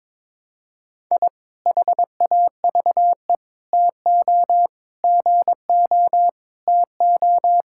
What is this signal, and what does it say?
Morse code, heard as a keyed tone, about 22 wpm, 700 Hz. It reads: I HAVE TO GO TO
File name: practice / 22wpm / Sentences with 3-Word Phrases - 22wpm